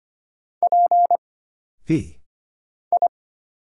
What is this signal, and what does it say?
Morse code, heard as a keyed tone, about 25 wpm, 700 Hz. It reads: P I
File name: Single Character (Letter:Number:Punctuation:Prosign) - Mind-Melt - 25wpm